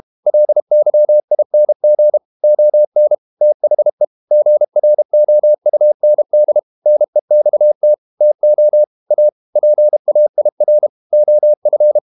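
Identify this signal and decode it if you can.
Morse code, heard as a keyed tone, about 32 wpm, 600 Hz. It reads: LYING ON THE GROUND NEXT TO A PAIR OF
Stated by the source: Morse code practice at 32 wpm from CW med Gnister, reading Ken Follett's novel Never in English.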